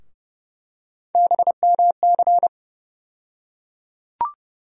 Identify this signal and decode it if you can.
Morse code, heard as a keyed tone, about 30 wpm, 700 Hz. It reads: BMC
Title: USA Call Sign Postfixes - 3 Letters 30wpm